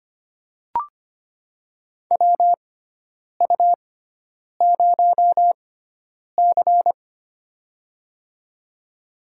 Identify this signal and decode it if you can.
Morse code, heard as a keyed tone, about 25 wpm, spaced as for 9 wpm, 700 Hz. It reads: WU0C
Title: USA Call Signs - 4 Characters - 6x Character Spacing - 25wpm